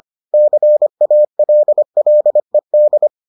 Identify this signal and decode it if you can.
Morse code, heard as a keyed tone, about 25 wpm, 600 Hz. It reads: CALLED